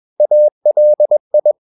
Morse code, heard as a keyed tone, about 21 wpm, 600 Hz. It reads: ALI
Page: Romanen Never af Ken Follet på engelsk.